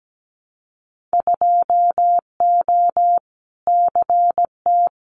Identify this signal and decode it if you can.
Morse code, heard as a keyed tone, about 17 wpm, 700 Hz. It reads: 2O CT
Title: Winter Field Day Exchanges 17wpm